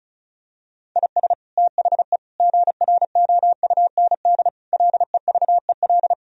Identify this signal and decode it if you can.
Morse code, heard as a keyed tone, about 36 wpm, 700 Hz. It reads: IS THE GROUND LEVEL
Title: Sentences from Top 1000 Words - Review Part 4 - 35wpm